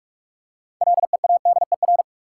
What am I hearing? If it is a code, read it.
Morse code, heard as a keyed tone, about 44 wpm, 700 Hz. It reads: LEADER
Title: Six Letter Words - 45wpm